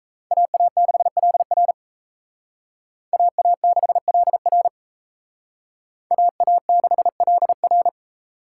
Morse code, heard as a keyed tone, about 36 wpm, 700 Hz. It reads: AA6LR AA6LR AA6LR